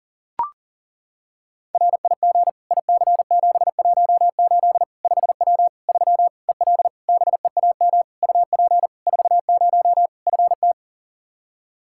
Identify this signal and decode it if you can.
Morse code, heard as a keyed tone, about 40 wpm, 700 Hz. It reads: RIG IC718 5W 3 EL BEAM UP 40 FT
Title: K1USN SST QSO Practice 40wpm